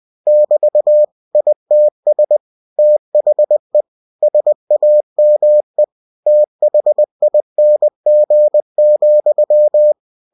Morse code, heard as a keyed tone, about 20 wpm, 600 Hz. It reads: <BT> ITS THE SAME THING,
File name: cw-books / winnie-the-pooh / winnie-20wpm-page-0110